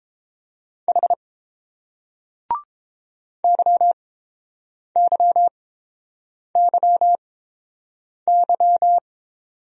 Morse code, heard as a keyed tone, about 29 wpm, spaced as for 7 wpm, 700 Hz. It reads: H YYYY